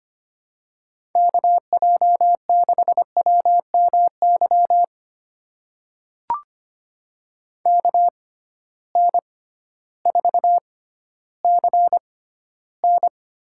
Morse code, heard as a keyed tone, about 25 wpm, spaced as for 13 wpm, 700 Hz. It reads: KJ6WMY KN4CN
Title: USA Call Signs - 6 Characters - 6x Character Spacing - 25wpm